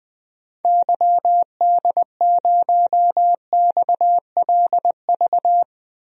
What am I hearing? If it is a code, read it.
Morse code, heard as a keyed tone, about 20 wpm, 700 Hz. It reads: YD0XLV
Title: DXCC-100 Call Signs - 6 Characters - Repeated 3x - 20wpm